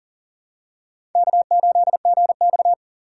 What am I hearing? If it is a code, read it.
Morse code, heard as a keyed tone, about 40 wpm, 700 Hz. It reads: K8GX
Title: USA Call Signs - 4 Characters - 3x Character Spacing - 40wpm